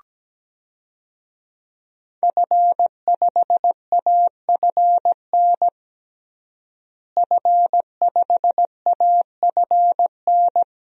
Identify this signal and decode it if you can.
Morse code, heard as a keyed tone, about 17 wpm, 700 Hz. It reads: F5AFN F5AFN
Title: DXCC-100 Call Signs - 5 Characters - Repeated 3x - 17wpm